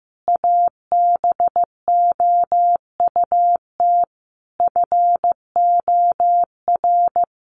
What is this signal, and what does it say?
Morse code, heard as a keyed tone, about 15 wpm, 700 Hz. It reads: ABOUT FOR
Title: A Sound Of Thunder 15wpm